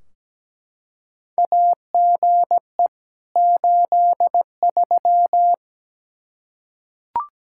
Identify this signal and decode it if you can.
Morse code, heard as a keyed tone, about 17 wpm, 700 Hz. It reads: AGE 83
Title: Practice QSO - Exchange 3 - More Info - 17wpm